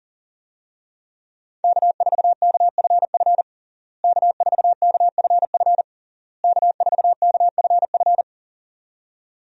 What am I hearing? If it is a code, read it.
Morse code, heard as a keyed tone, about 40 wpm, 700 Hz. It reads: K4KFF K4KFF K4KFF
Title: USA Call Signs - 5 Characters - Repeated 3x - 40wpm